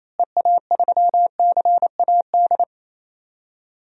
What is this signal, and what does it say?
Morse code, heard as a keyed tone, about 28 wpm, 700 Hz. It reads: EA3CAD
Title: DXCC-100 Call Signs - 6 Characters - Repeated 3x - 28wpm